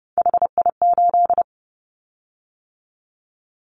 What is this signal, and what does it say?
Morse code, heard as a keyed tone, about 30 wpm, 700 Hz. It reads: HI8
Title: Callsign Prefixes for Top 100 DXCC - 30wpm